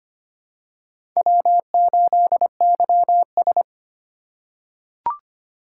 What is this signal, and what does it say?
Morse code, heard as a keyed tone, about 25 wpm, 700 Hz. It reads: W8YS
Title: USA Call Signs - 4x Character Spacing - 25wpm